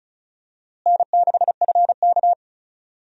Morse code, heard as a keyed tone, about 36 wpm, 700 Hz. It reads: N6FK
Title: USA Call Signs - 4 Characters - Repeated 3x - 35wpm